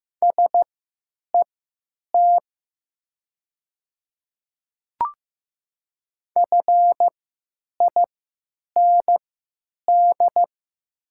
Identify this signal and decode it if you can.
Morse code, heard as a keyed tone, about 15 wpm, spaced as for 8 wpm, 700 Hz. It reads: SET FIND
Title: 3x Character Spacing - Top 200 Words - 15wpm